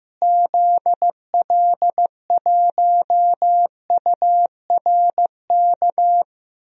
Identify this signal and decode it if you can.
Morse code, heard as a keyed tone, about 15 wpm, 700 Hz. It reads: ZL1URK
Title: DXCC-100 Call Signs - 6 Characters - Repeated 3x - 15wpm